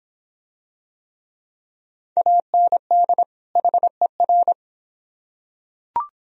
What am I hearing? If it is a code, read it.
Morse code, heard as a keyed tone, about 24 wpm, 700 Hz. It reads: AND HER